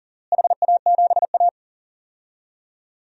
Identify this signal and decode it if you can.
Morse code, heard as a keyed tone, about 40 wpm, 700 Hz. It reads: HA7A
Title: DXCC-100 Call Signs - 4 Characters - Repeated 3x - 40wpm